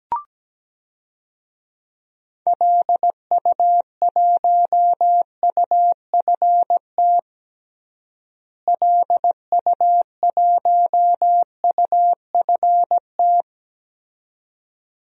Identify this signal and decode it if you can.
Morse code, heard as a keyed tone, about 17 wpm, 700 Hz. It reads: LU1UFT LU1UFT
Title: DXCC-100 Call Signs - 6 Characters - Repeated 2x - 17wpm